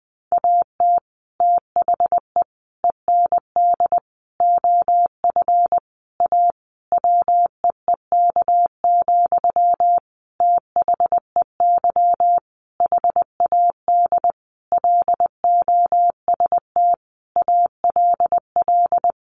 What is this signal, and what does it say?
Morse code, heard as a keyed tone, about 20 wpm, 700 Hz. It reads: AT THE END OF A WEEK, THEY HAD LOST ALL
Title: Sentences from Top 1400 Words 20wpm